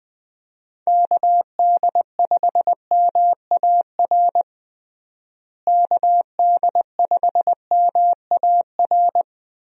Morse code, heard as a keyed tone, about 20 wpm, 700 Hz. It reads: KD5MAR KD5MAR
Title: USA Call Signs - 6 Characters - Repeated 3x - 20wpm